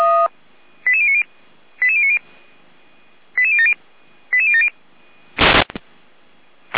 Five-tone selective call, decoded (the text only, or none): 88090 88090 88070 88070